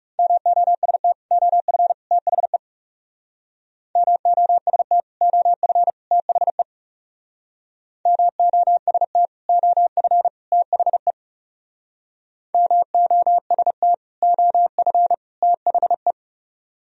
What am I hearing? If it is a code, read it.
Morse code, heard as a keyed tone, about 37 wpm, 700 Hz. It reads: MOST OF THE MOST OF THE MOST OF THE MOST OF THE
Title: Speed-Racing - Sets of 3 Words 30wpm